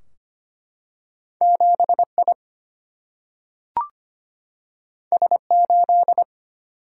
Morse code, heard as a keyed tone, about 25 wpm, 700 Hz. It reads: 7I S8
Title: Single-Letter Single-Number 25wpm